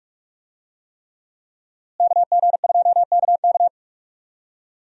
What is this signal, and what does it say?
Morse code, heard as a keyed tone, about 41 wpm, 700 Hz. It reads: KG2KK
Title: USA Call Signs - 5 Characters - Repeated 3x - 45wpm